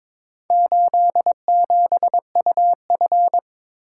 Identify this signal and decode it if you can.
Morse code, heard as a keyed tone, about 21 wpm, 700 Hz. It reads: 87UF
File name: practice / 22wpm / Four Letter-Number Combinations - 22wpm